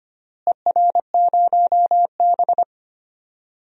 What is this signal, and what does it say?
Morse code, heard as a keyed tone, about 25 wpm, 700 Hz. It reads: ER0B
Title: DXCC-100 Call Signs - 4 Characters - Repeated 3x - 25wpm